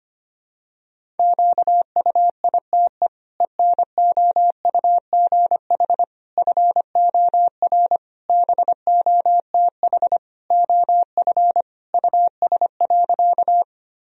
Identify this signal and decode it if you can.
Morse code, heard as a keyed tone, about 25 wpm, 700 Hz. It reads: QUITE ENOUGH FOR BOTH OF US.
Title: Sentences from Top 1000 Words - Review Part 1 - 25wpm